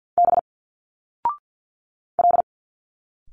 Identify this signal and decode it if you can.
Morse code, heard as a keyed tone, about 50 wpm, 700 Hz. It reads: B L
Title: Single Letters 50wpm